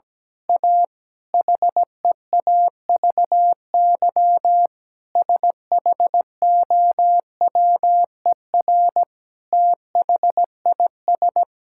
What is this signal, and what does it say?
Morse code, heard as a keyed tone, about 17 wpm, 700 Hz. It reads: A HEAVY SHOWER THIS
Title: Sentences from Top 1600 Words 17wpm